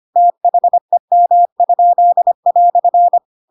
Morse code, heard as a keyed tone, about 25 wpm, 700 Hz. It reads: THEM?"
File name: Pinocchio / Pinocchio ch7 25 25 ik1hge